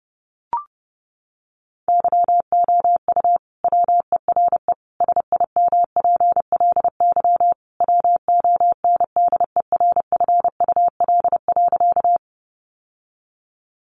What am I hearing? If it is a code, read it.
Morse code, heard as a keyed tone, about 28 wpm, 700 Hz. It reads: YOU WERE SIMPLY WONDERFUL.
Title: Sentences from Top 1100 Words 30wpm